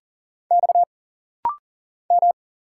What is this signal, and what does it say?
Morse code, heard as a keyed tone, about 40 wpm, 700 Hz. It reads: X M
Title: Single Letter-Number Alt - Rapid-Fire 40wpm